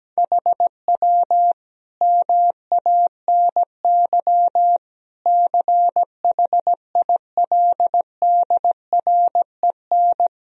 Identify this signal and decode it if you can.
Morse code, heard as a keyed tone, about 17 wpm, 700 Hz. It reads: HW MANY CHILDREN
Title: Sentences from Top 400 Words and Abbreviations 17wpm